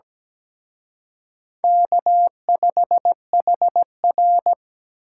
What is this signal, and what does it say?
Morse code, heard as a keyed tone, about 20 wpm, 700 Hz. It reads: K5HR
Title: Speed Racing - USA 1x2 Call Signs 17wpm